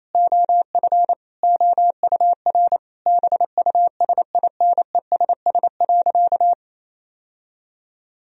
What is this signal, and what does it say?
Morse code, heard as a keyed tone, about 28 wpm, 700 Hz. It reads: OF OUR BUSINESS.
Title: Sentences from Top 1000 Words - Review Part 4 - 28wpm